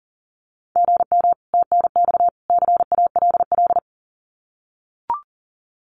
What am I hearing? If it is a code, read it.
Morse code, heard as a keyed tone, about 39 wpm, 700 Hz. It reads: GM TNX CALL
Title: Common QSO Phrases 40wpm